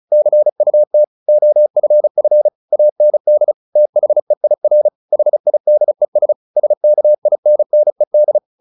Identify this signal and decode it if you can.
Morse code, heard as a keyed tone, about 34 wpm, 600 Hz. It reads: CUT OFF AND THEIR HIDES SKINNED
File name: the old man and the sea / the old man and the sea 1 35wpm